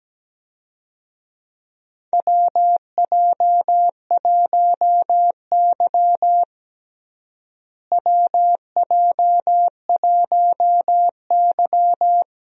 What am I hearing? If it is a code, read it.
Morse code, heard as a keyed tone, about 17 wpm, 700 Hz. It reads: WJ1Y WJ1Y
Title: USA Call Signs - 4 Characters - Repeated 3x - 17wpm